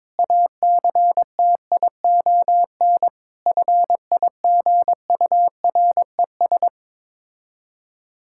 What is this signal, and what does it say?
Morse code, heard as a keyed tone, about 22 wpm, 690 Hz. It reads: ACTION FIGURES
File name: Difficult Words with Hints - 22wpm